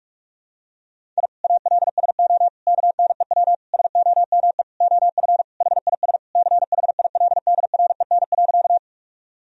Morse code, heard as a keyed tone, about 42 wpm, 700 Hz. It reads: I ALSO KNEW SOME OF HIS CHILDREN.
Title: Sentences from Top 1000 Words - Review Part 1 - 45wpm